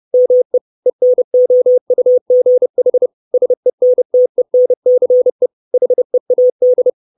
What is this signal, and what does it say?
Morse code, heard as a keyed tone, about 30 wpm, 500 Hz. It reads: ME ENOUGH SENTENCE HEAD
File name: 632 Words 30WPM